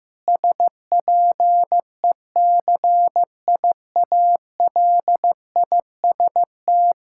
Morse code, heard as a keyed tone, about 15 wpm, 700 Hz. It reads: SPECIALIST
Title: Practice QSO - Exchange 3 - More Info - 15wpm